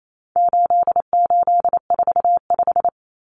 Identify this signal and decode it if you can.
Morse code, heard as a keyed tone, about 28 wpm, 700 Hz. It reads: 8845
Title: Numbers - 4-Digit 28wpm